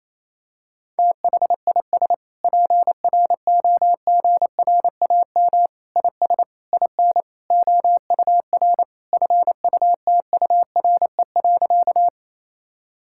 Morse code, heard as a keyed tone, about 28 wpm, 700 Hz. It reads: THIS PROGRAM IS IN OUR FUTURE.
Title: Sentences from Top 1000 Words - Review Part 5 - 28wpm